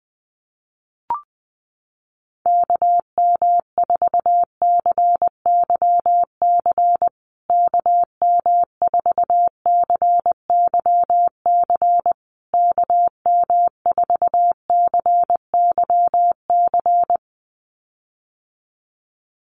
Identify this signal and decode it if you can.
Morse code, heard as a keyed tone, about 20 wpm, 700 Hz. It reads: KM4CYC KM4CYC KM4CYC
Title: USA Call Signs Repeated 3x 20wpm